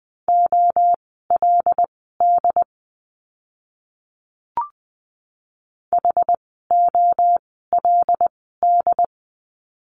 Morse code, heard as a keyed tone, about 20 wpm, spaced as for 14 wpm, 700 Hz. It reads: OLD HOLD
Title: 101 Lesson 17f - Letter u - Words 20wpm